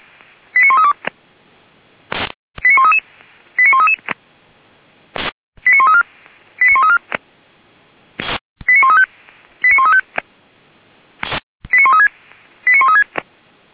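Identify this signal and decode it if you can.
Five-tone selective call (ZVEI), heard as quick runs of five tones, tones of about 70 ms. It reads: 89132 89133 89133 89134 89134 89135 89135 89136 89136
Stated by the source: ZVEI five-tone call 89131 received by SDR.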